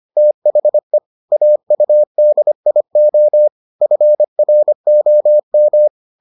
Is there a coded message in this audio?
Yes: Morse code from 0.2 to 5.9 s, heard as a keyed tone, about 25 wpm, 600 Hz. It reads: THE AUDIO FROM